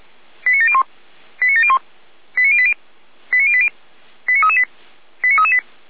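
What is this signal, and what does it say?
Five-tone selective call (ZVEI), heard as quick runs of five tones, tones of about 70 ms. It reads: 89701 89701 89080 89080 89338 89338
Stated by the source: ZVEI five-tone call received by SDR.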